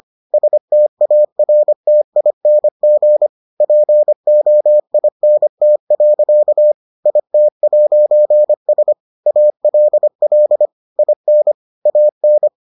Morse code, heard as a keyed tone, about 25 wpm, 600 Hz. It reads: STARTING POINT. IT'S ALL IN AN